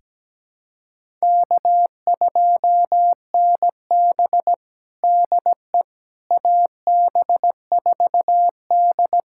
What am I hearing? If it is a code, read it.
Morse code, heard as a keyed tone, about 17 wpm, 700 Hz. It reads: K2NB DE AB4D
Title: USA Callsign DE Callsign - 4 Characters - 17wpm